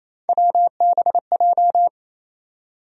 Morse code, heard as a keyed tone, about 28 wpm, 700 Hz. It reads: WBJ
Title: USA Call Sign Postfixes - 3 Letters 28wpm